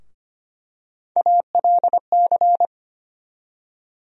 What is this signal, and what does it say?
Morse code, heard as a keyed tone, about 25 wpm, 700 Hz. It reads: ALC
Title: USA Call Sign Postfixes - 3 Letters 25wpm